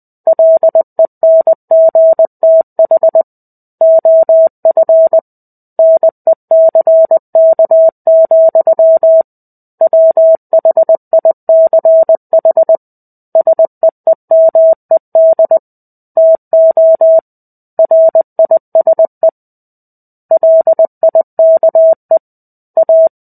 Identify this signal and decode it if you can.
Morse code, heard as a keyed tone, about 20 wpm, 650 Hz. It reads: LENGTH OF NECK, WHICH SEEMED TO RISE LIKE A
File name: 05-Chapter-05 20 WPM